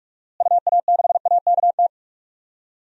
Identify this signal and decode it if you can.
Morse code, heard as a keyed tone, about 46 wpm, 700 Hz. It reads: UA6AKT
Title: DXCC-100 Call Signs - 6 Characters - Repeated 3x - 45wpm